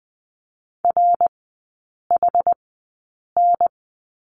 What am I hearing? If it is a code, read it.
Morse code, heard as a keyed tone, about 20 wpm, spaced as for 8 wpm, 700 Hz. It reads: RHN
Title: 152 Lesson 25e - Letter p - Three Characters 20wpm